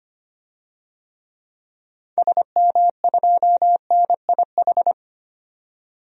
Morse code, heard as a keyed tone, about 25 wpm, 700 Hz. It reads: SM2NIH
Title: DXCC-100 Call Signs - 6 Characters - 25wpm